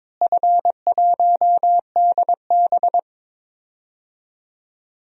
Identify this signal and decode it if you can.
Morse code, heard as a keyed tone, about 22 wpm, 700 Hz. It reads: F1DB
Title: DXCC-100 Call Signs - 4 Characters - Repeated 2x - 22wpm